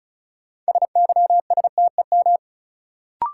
Morse code, heard as a keyed tone, about 36 wpm, 700 Hz. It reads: SYSTEM
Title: Sentences from Top 1000 Words - Review Part 3 - New Word Drill - 35wpm